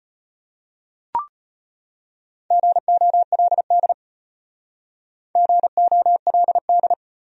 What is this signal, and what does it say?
Morse code, heard as a keyed tone, about 34 wpm, 700 Hz. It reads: GOLD GOLD